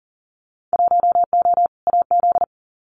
Morse code, heard as a keyed tone, about 37 wpm, 700 Hz. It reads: 1O AZ
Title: Winter Field Day Exchanges 40wpm